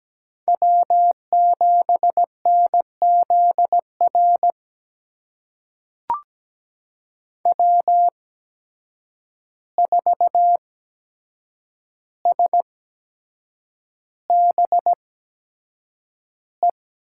Morse code, heard as a keyed tone, about 17 wpm, spaced as for 7 wpm, 700 Hz. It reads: W7NZR W4SBE
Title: USA Call Signs - 5 Characters - 8x Character Spacing - 17wpm